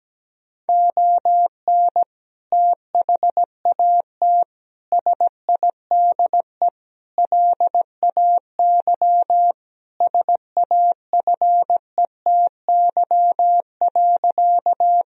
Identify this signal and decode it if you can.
Morse code, heard as a keyed tone, about 17 wpm, 700 Hz. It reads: ON THAT SIDE LAY SAFETY.
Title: Sentences from Top 1000 Words - Review Part 4 - 17wpm